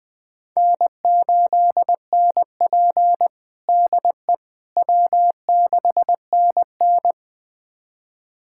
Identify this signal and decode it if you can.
Morse code, heard as a keyed tone, about 20 wpm, 700 Hz. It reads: N8NP DE W6NN
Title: USA Callsign DE Callsign - 4 Characters - 20wpm